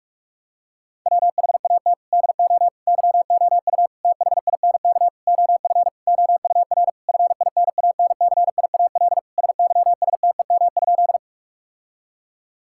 Morse code, heard as a keyed tone, about 42 wpm, 700 Hz. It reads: WHAT DO YOU THINK OF OUR FINANCIAL SYSTEM?
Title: Sentences from Top 1000 Words - Review Part 5 - 45wpm